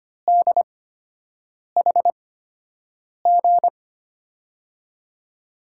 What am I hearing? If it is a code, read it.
Morse code, heard as a keyed tone, about 25 wpm, spaced as for 7 wpm, 700 Hz. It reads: DHG
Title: USA Call Signs - 8x Character Spacing - 25wpm